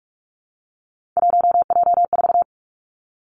Morse code, heard as a keyed tone, about 46 wpm, 700 Hz. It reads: 1J4
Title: Three Challenging Letter-Number Combinations 45wpm